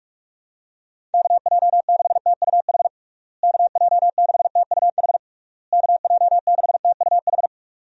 Morse code, heard as a keyed tone, about 46 wpm, 700 Hz. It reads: KJ6TUH KJ6TUH KJ6TUH
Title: USA Call Signs - 6 Characters - Repeated 3x - 45wpm